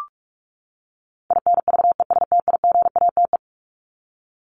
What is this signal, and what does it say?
Morse code, heard as a keyed tone, about 44 wpm, 700 Hz. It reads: INVESTIGATE
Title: Top 1751-2000 Words 45wpm